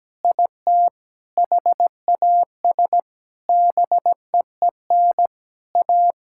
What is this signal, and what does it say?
Morse code, heard as a keyed tone, about 17 wpm, 700 Hz. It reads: IT HAS BEEN A
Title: Sentences from Top 1000 Words - Review Part 6 - 17wpm